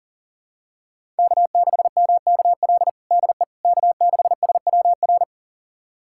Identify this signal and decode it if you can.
Morse code, heard as a keyed tone, about 40 wpm, 700 Hz. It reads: K6MKL DE K6SWR
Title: USA Callsign DE Callsign - 5 Characters - 40wpm